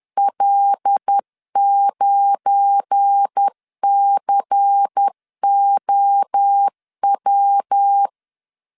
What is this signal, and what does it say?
Morse code, heard as a keyed tone, about 10 wpm, 800 Hz. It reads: L9COW
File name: LEZIONI / 7 GF9Z / 7 GF9 10wpm 2747